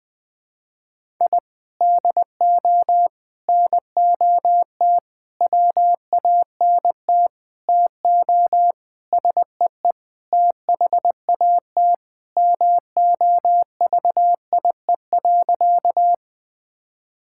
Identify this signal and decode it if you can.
Morse code, heard as a keyed tone, about 19 wpm, 700 Hz. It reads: I DO NOT WANT TO SEE THAT MOVIE.
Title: Sentences with 5-Word Phrases - 20wpm